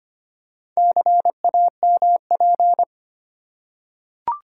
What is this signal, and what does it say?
Morse code, heard as a keyed tone, about 25 wpm, 700 Hz. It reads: CAMP
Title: Four Letter Words - 25wpm